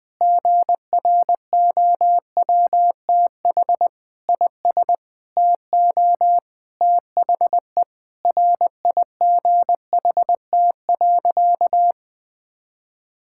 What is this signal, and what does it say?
Morse code, heard as a keyed tone, about 20 wpm, 700 Hz. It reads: GROWTH IS TO THE RIGHT.
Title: Sentences from Top 1000 Words - Review Part 5 - 20wpm